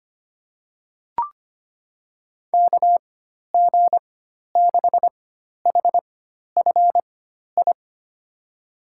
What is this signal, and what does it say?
Morse code, heard as a keyed tone, about 25 wpm, spaced as for 12 wpm, 700 Hz. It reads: KG6HFI
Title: USA Call Signs - 6 Characters - 4x Character Spacing - 25wpm